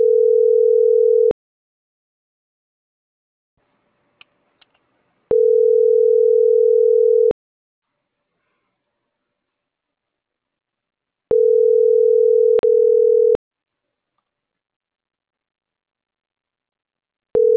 Ringback tone (North America), 440 + 480 Hz.